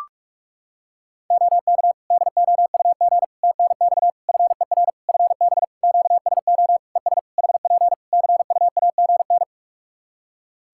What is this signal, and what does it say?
Morse code, heard as a keyed tone, about 46 wpm, 700 Hz. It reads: OK DOUG TNX FER FB QSO ES HP CUAGN
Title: K1USN SST QSO Practice 45wpm